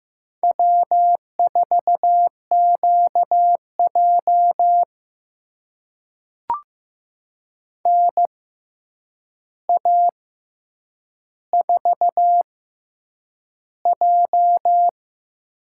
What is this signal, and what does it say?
Morse code, heard as a keyed tone, about 15 wpm, spaced as for 8 wpm, 700 Hz. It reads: W4QJ NA4J
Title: USA Call Signs - 4 Characters - 6x Character Spacing - 15wpm